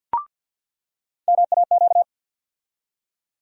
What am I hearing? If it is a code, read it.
Morse code, heard as a keyed tone, about 47 wpm, 700 Hz. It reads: MAQ
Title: USA Call Sign Postfixes - 3 Letters 50wpm